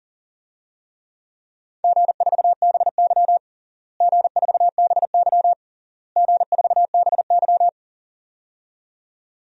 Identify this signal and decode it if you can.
Morse code, heard as a keyed tone, about 40 wpm, 700 Hz. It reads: G4BY G4BY G4BY
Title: DXCC-100 Call Signs - 4 Characters - Repeated 3x - 40wpm